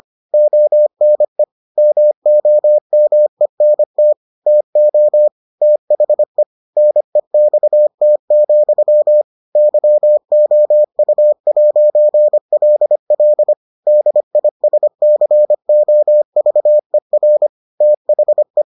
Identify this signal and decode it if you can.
Morse code, heard as a keyed tone, about 25 wpm, 600 Hz. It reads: ONE MOMENT TO THE NEXT, YOU'LL DISCOVER THE